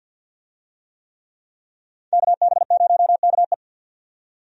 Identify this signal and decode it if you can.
Morse code, heard as a keyed tone, about 49 wpm, 700 Hz. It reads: KB0KE
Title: USA Call Signs - 5 Characters - 50wpm